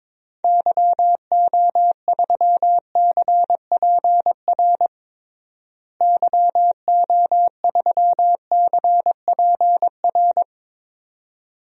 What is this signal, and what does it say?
Morse code, heard as a keyed tone, about 22 wpm, 700 Hz. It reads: YO3CPR YO3CPR
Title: DXCC-100 Call Signs - 6 Characters - Repeated 2x - 22wpm